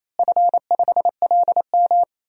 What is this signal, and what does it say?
Morse code, heard as a keyed tone, about 28 wpm, 700 Hz. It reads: F5LM
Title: DXCC-100 Call Signs - 4 Characters - Repeated 3x - 28wpm